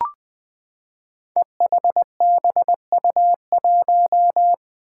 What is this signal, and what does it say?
Morse code, heard as a keyed tone, about 20 wpm, 700 Hz. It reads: EHBU1
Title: Five Challenging Letter-Number Combinations 20wpm